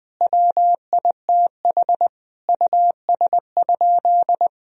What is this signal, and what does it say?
Morse code, heard as a keyed tone, about 21 wpm, 700 Hz. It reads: WITH US?
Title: Sentences from Top 1000 Words - Review Part 3 - 20wpm